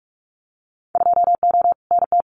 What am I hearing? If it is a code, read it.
Morse code, heard as a keyed tone, about 44 wpm, 700 Hz. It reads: 2O NT
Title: Winter Field Day Exchanges Encore 45wpm